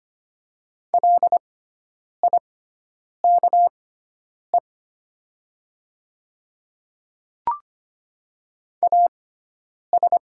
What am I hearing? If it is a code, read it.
Morse code, heard as a keyed tone, about 25 wpm, spaced as for 9 wpm, 700 Hz. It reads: LIKE AS
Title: 6x Character Spacing - Top 100 Words - 25wpm